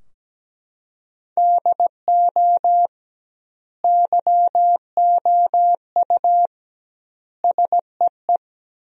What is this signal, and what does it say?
Morse code, heard as a keyed tone, about 17 wpm, 700 Hz. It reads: DO YOU SEE
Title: 2x Word-Spacing - Sentences from Top 100 Words - 17wpm